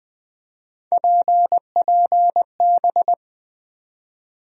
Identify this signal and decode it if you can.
Morse code, heard as a keyed tone, about 20 wpm, 700 Hz. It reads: PPB
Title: USA Call Sign Postfixes - 3 Letters 20wpm